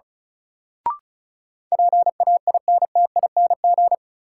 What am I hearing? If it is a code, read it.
Morse code, heard as a keyed tone, about 36 wpm, 700 Hz. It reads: PAINTING
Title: Sentences from Top 1000 Words - Review Part 6 - New Word Drill - 35wpm